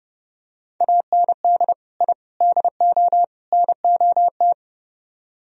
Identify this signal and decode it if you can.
Morse code, heard as a keyed tone, about 32 wpm, 700 Hz. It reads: AND I DO NOT